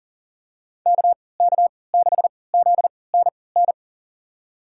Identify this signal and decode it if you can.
Morse code, heard as a keyed tone, about 40 wpm, spaced as for 23 wpm, 700 Hz. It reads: KK6ZNN
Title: USA Call Signs - 3x Character Spacing - 40wpm